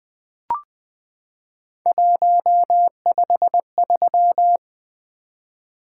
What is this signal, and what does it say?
Morse code, heard as a keyed tone, about 20 wpm, 700 Hz. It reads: 153
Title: Three Letter-Number Combinations - 20wpm